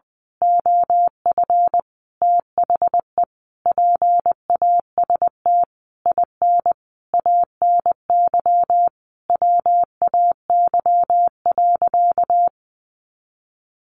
Morse code, heard as a keyed tone, about 20 wpm, 700 Hz. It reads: OF THE PAST IN ANY WAY.
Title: A Sound Of Thunder 20wpm